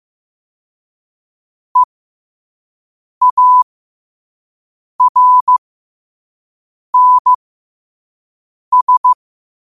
Morse code, heard as a keyed tone, about 15 wpm, spaced as for 5 wpm, 1000 Hz. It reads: EARNS
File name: qotdfiles / 5wpm